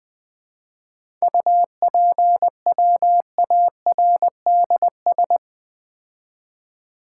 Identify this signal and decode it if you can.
Morse code, heard as a keyed tone, about 20 wpm, 700 Hz. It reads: UPWARDS
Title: 7 Letter Words - First 3 Letters Given - 20wpm